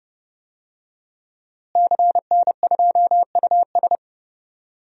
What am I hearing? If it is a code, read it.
Morse code, heard as a keyed tone, about 32 wpm, 700 Hz. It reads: CN2US
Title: DXCC-100 Call Signs - 5 Characters - Repeated 2x - 30wpm